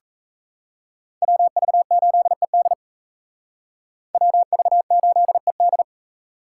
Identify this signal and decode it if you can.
Morse code, heard as a keyed tone, about 35 wpm, 700 Hz. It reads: WV8ED WV8ED
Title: Speed Racing - USA 2x2 Call Signs 28wpm